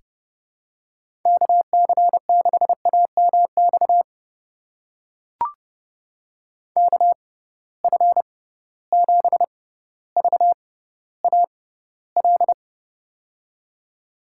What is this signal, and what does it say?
Morse code, heard as a keyed tone, about 31 wpm, spaced as for 16 wpm, 700 Hz. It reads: KC6AMX KF7VAL
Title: USA Call Signs - 6x Character Spacing - 30wpm